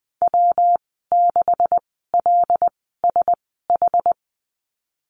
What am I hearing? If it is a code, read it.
Morse code, heard as a keyed tone, about 19 wpm, spaced as for 14 wpm, 700 Hz. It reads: W6LSH
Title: 140 Lesson 23g - Letter f - Call Signs 20wpm